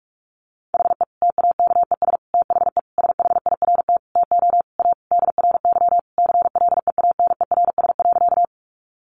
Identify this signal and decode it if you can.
Morse code, heard as a keyed tone, about 44 wpm, 700 Hz. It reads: HE TAKES THE SHIRT TO A DRY CLEANERS.
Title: Additional Easy Phrases 45wpm